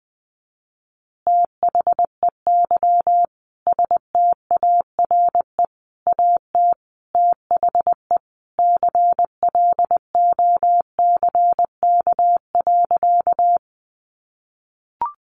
Morse code, heard as a keyed tone, about 20 wpm, 700 Hz. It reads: THEY STARE AT THE CLOCK.
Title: Easy Phrases 20wpm